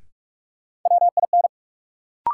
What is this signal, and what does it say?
Morse code, heard as a keyed tone, about 44 wpm, 700 Hz. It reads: WIN
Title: Sentences from Top 1500 Words - New Word Drill - 45wpm